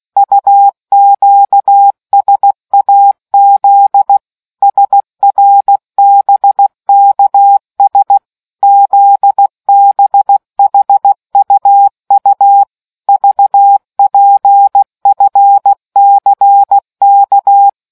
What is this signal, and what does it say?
Morse code, heard as a keyed tone, about 16 wpm, 800 Hz. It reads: UQSAZ SRBKS ZBHUU VPFCK